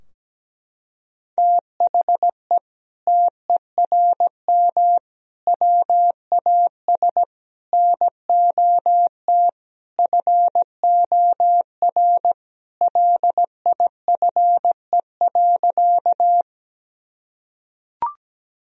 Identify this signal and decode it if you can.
Morse code, heard as a keyed tone, about 17 wpm, 700 Hz. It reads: THE TERM WAS NOT FOR LIFE.
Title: Sentences from Top 1000 Words - Review Part 4 - 17wpm